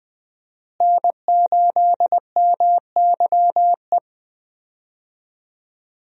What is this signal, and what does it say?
Morse code, heard as a keyed tone, about 20 wpm, 700 Hz. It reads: N8MYE
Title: USA Call Signs - 5 Characters - Repeated 2x - 20wpm